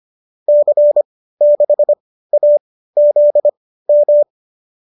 Morse code, heard as a keyed tone, about 25 wpm, spaced as for 15 wpm, 590 Hz. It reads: C6AZM
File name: qso204 15